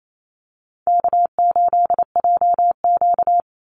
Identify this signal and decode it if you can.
Morse code, heard as a keyed tone, about 28 wpm, 700 Hz. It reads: K8JQ
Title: Easy Call Signs - 28wpm